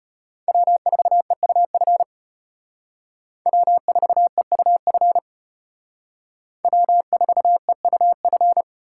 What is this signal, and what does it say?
Morse code, heard as a keyed tone, about 33 wpm, 700 Hz. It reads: W4EUF W4EUF W4EUF